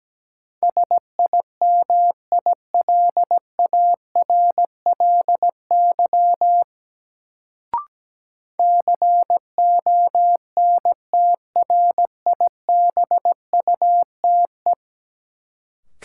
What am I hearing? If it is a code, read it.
Morse code, heard as a keyed tone, about 17 wpm, 700 Hz. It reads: SIMILARLY CONTRIBUTE
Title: Sentences from Top 1400 Words - New Word Drill - 17wpm